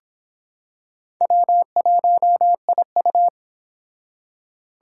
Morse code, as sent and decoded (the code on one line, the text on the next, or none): .-- .---- .. ..-
W1IU